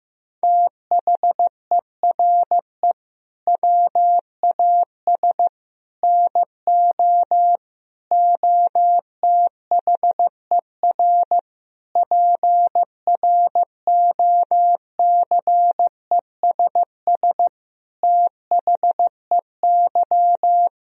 Morse code, heard as a keyed tone, about 16 wpm, 700 Hz. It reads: THERE WAS NO OTHER PROCESS THEY
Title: Sentences from Top 1000 Words - Review Part 3 - 15wpm